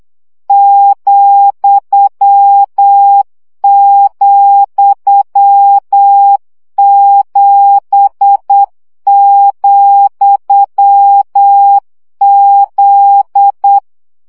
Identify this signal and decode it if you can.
Morse code, heard as a keyed tone, about 8 wpm, 800 Hz. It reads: ,,7,Z
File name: CW gyakorlas z 7 vesszo 7 baud